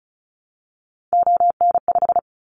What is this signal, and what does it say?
Morse code, heard as a keyed tone, about 36 wpm, 700 Hz. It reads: ON5
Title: Callsign Prefixes for Top 25 DXCC - 35wpm